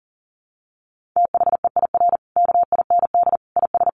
Morse code, heard as a keyed tone, about 40 wpm, 700 Hz. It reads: THEIR KIND IS
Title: Sentences from Top 1500 Words 40wpm